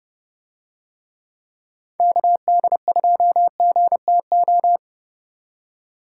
Morse code, heard as a keyed tone, about 30 wpm, 700 Hz. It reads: KD2GTO